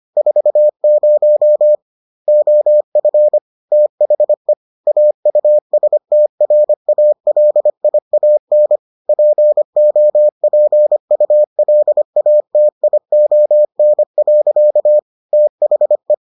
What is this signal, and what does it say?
Morse code, heard as a keyed tone, about 25 wpm, 600 Hz. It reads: 40 OF THE AUSTRALIAN POPULATION. THE